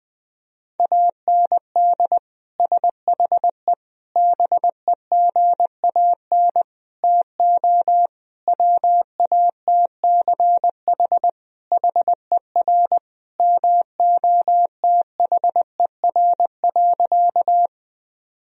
Morse code, heard as a keyed tone, about 21 wpm, 700 Hz. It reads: AND SHE BEGAN TO WATCH HER MOTHER.
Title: Sentences from Top 1000 Words - Review Part 1 - 20wpm